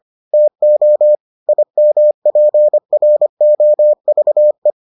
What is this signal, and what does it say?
Morse code, heard as a keyed tone, about 25 wpm, 600 Hz. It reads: TO IMPROVE